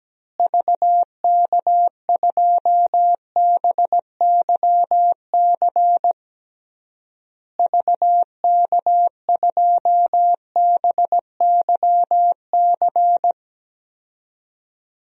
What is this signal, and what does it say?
Morse code, heard as a keyed tone, about 17 wpm, 700 Hz. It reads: VK2BYC VK2BYC
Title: DXCC-100 Call Signs - 6 Characters - Repeated 3x - 17wpm